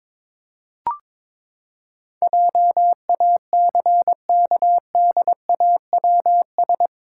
Morse code, heard as a keyed tone, about 22 wpm, 700 Hz. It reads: JACKDAWS
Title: Pangram Challenge v2 - 22wpm